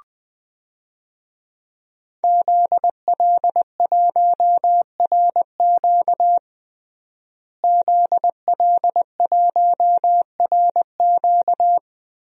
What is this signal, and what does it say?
Morse code, heard as a keyed tone, about 20 wpm, 700 Hz. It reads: ZL1RQ ZL1RQ
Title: DXCC-100 Call Signs - 5 Characters - Repeated 2x - 20wpm